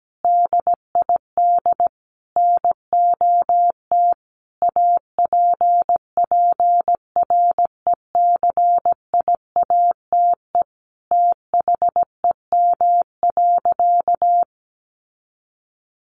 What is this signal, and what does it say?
Morse code, heard as a keyed tone, about 17 wpm, 700 Hz. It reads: DID NOT APPRECIATE THEM.
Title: Sentences from Top 1300 Words 17wpm